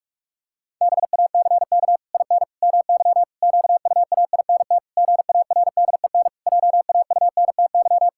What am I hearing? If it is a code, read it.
Morse code, heard as a keyed tone, about 46 wpm, 700 Hz. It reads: BACK IN MY QUAINT GARDEN JAUNTY
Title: Pangram Challenge v2 - 45wpm